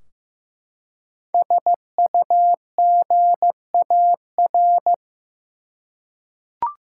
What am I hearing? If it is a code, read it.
Morse code, heard as a keyed tone, about 15 wpm, 700 Hz. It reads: SUGAR
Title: Five Letter Words - 15wpm